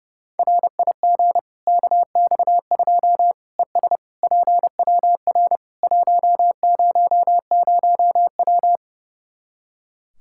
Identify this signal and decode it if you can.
Morse code, heard as a keyed tone, about 32 wpm, 700 Hz. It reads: RIG KX2 ES PWR 100W
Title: Practice QSO - Exchange 2 - Info - 30wpm